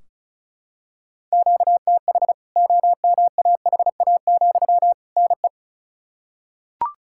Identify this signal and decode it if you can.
Morse code, heard as a keyed tone, about 34 wpm, 700 Hz. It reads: QTH OMAHA, NE
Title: K1USN SST QSO Practice 35wpm